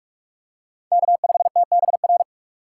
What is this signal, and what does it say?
Morse code, heard as a keyed tone, about 46 wpm, 700 Hz. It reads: K5TBR
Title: USA Call Signs - 5 Characters - 6x Character Spacing - 45wpm